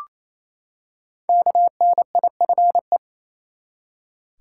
Morse code, heard as a keyed tone, about 28 wpm, 700 Hz. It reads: KNIFE
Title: Five Letter Words - 28wpm